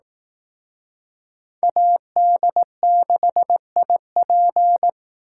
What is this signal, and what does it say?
Morse code, heard as a keyed tone, about 18 wpm, 700 Hz. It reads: AD6IP
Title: Speed Racing - USA 2x2 Call Signs 15wpm